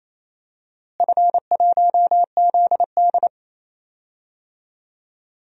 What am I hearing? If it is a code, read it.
Morse code, heard as a keyed tone, about 28 wpm, 700 Hz. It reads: F1ZD
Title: DXCC-100 Call Signs - 4 Characters - Repeated 2x - 28wpm